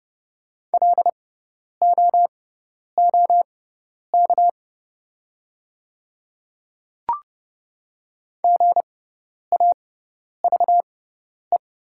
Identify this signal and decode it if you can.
Morse code, heard as a keyed tone, about 30 wpm, spaced as for 10 wpm, 690 Hz. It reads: LOOK GAVE
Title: 6x Character Spacing - Top 500 Words - 30wpm